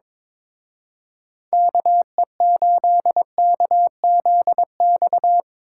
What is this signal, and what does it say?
Morse code, heard as a keyed tone, about 24 wpm, 700 Hz. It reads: KE8KZX
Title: Speed Racing - USA 2x3 Call Signs 22wpm